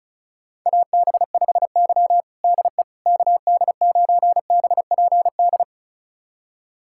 Morse code, heard as a keyed tone, about 36 wpm, 690 Hz. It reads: AB5Y DE KD9BPD
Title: USA Callsign DE Callsign 35wpm